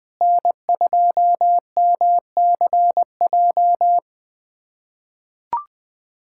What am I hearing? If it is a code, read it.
Morse code, heard as a keyed tone, about 20 wpm, 700 Hz. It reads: N2MCJ
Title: USA Callsign DE Callsign 20wpm